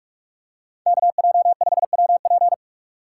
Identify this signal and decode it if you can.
Morse code, heard as a keyed tone, about 44 wpm, 700 Hz. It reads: KJ5WP
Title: USA Call Signs - 5 Characters - 6x Character Spacing - 45wpm